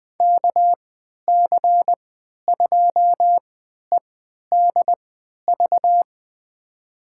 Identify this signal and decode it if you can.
Morse code, heard as a keyed tone, about 20 wpm, spaced as for 11 wpm, 700 Hz. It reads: KC2EDV
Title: USA Call Signs - 6 Characters - 3x Character Spacing - 20wpm